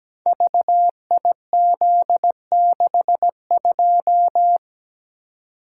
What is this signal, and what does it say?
Morse code, heard as a keyed tone, about 17 wpm, 700 Hz. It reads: VIZ62
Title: Code Groups - 5 Letter:Numbers - 17wpm